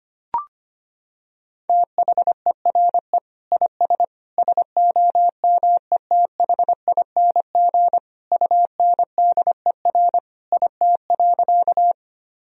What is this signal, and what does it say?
Morse code, heard as a keyed tone, about 25 wpm, 700 Hz. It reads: THERE IS SOMETHING UNDER IT.
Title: Sentences from Top 1000 Words - Review Part 1 - 25wpm